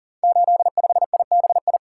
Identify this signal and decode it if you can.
Morse code, heard as a keyed tone, about 39 wpm, 700 Hz. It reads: 85IBI
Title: Code Groups - 5 Letter:Numbers - 40wpm